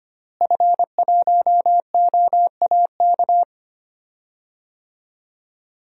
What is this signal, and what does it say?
Morse code, heard as a keyed tone, about 25 wpm, 700 Hz. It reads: F1OAK
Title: DXCC-100 Call Signs - 5 Characters - Repeated 3x - 25wpm